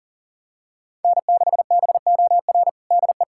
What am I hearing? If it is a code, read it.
Morse code, heard as a keyed tone, about 37 wpm, 700 Hz. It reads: N6BOR DE
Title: USA Callsign DE Callsign - 5 Characters - 40wpm